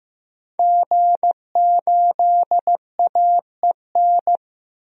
Morse code, heard as a keyed tone, about 15 wpm, 700 Hz. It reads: G8AEN
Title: DXCC-100 Call Signs - 5 Characters - 15wpm